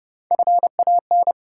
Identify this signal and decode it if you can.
Morse code, heard as a keyed tone, about 32 wpm, 700 Hz. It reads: FAN
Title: Sentences from Top 1000 Words - Review Part 4 - New Word Drill - 30wpm